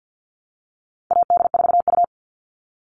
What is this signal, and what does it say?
Morse code, heard as a keyed tone, about 50 wpm, 700 Hz. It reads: AD4U
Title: All Cut Numbers - 4-Digit 50wpm